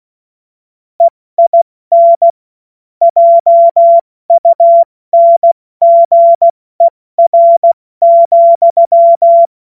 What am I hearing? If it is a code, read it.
Morse code, heard as a keyed tone, about 16 wpm, 680 Hz. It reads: EIN JUNGER,
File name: Morsekurs / Witze / Witz 02. August 2022 70 F80